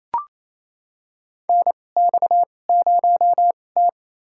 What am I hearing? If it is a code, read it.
Morse code, heard as a keyed tone, about 28 wpm, spaced as for 20 wpm, 700 Hz. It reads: NX0T
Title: USA Call Signs - 4 Characters - 2x Character Spacing - 28wpm